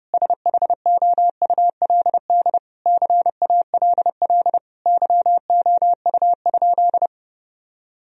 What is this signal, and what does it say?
Morse code, heard as a keyed tone, about 30 wpm, 700 Hz. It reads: SHOULD CALL YOU?
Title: Sentences from Top 1000 Words - Review Part 3 - 30wpm